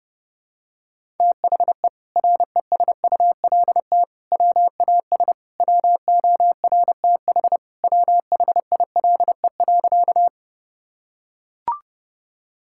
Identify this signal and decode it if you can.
Morse code, heard as a keyed tone, about 30 wpm, 700 Hz. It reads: THE RESULT WAS WORTH WHILE.
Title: Sentences from Top 1000 Words - Review Part 5 - 30wpm